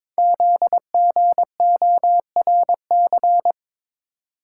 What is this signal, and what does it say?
Morse code, heard as a keyed tone, about 22 wpm, 700 Hz. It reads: ZGORC
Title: Code Groups - 5 Letters - 22wpm